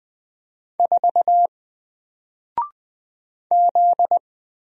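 Morse code, heard as a keyed tone, about 20 wpm, 700 Hz. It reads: 4 Z
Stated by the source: Single letter and number Morse drill at 20 wpm, from N4LCD.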